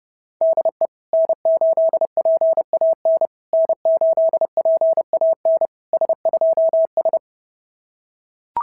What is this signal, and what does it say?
Morse code, heard as a keyed tone, about 30 wpm, 650 Hz. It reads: DE N8PAN N8PAN S2S
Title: SOTA QSO Practice 30wpm